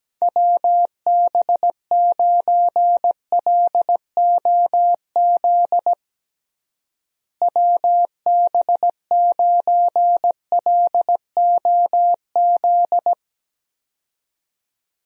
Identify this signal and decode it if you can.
Morse code, heard as a keyed tone, about 17 wpm, 700 Hz. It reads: WB9LOZ WB9LOZ
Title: USA Call Signs - 6 Characters - Repeated 2x - 17wpm